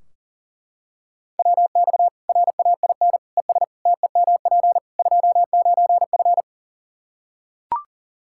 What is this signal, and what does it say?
Morse code, heard as a keyed tone, about 40 wpm, 700 Hz. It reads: WX RAIN ES TEMP 29F
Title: Practice QSO - Exchange 2 - Info - 40wpm